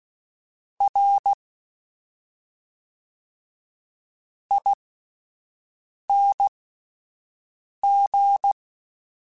Morse code, heard as a keyed tone, about 16 wpm, spaced as for 5 wpm, 800 Hz. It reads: R ING